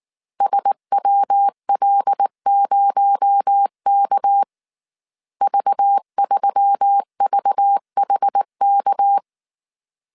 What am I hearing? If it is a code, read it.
Morse code, heard as a keyed tone, about 19 wpm, 800 Hz. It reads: SWL0K V3VHK
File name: CW / DET / 1 6 1500C S A 18wpm 2255